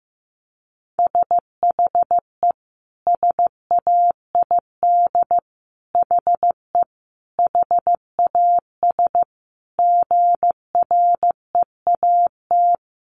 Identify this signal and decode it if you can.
Morse code, heard as a keyed tone, about 15 wpm, 700 Hz. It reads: SHE SAID HE HAS GREAT
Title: Sentences from Top 1300 Words 15wpm